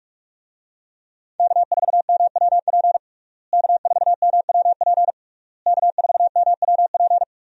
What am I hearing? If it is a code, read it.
Morse code, heard as a keyed tone, about 41 wpm, 700 Hz. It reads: K4MWP K4MWP K4MWP
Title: USA Call Signs - 5 Characters - Repeated 3x - 45wpm